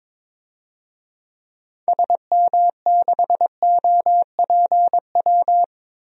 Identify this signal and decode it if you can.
Morse code, heard as a keyed tone, about 22 wpm, 700 Hz. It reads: SM6OPW
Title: DXCC-100 Call Signs - 6 Characters - 22wpm